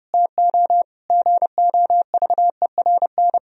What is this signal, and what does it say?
Morse code, heard as a keyed tone, about 32 wpm, 700 Hz. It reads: TO GOVERN
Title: Sentences from Top 1000 Words - Review Part 2 - 30wpm